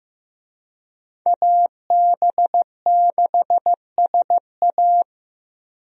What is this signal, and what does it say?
Morse code, heard as a keyed tone, about 15 wpm, 700 Hz. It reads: AB6SA
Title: USA Call Signs - 5 Characters - 15wpm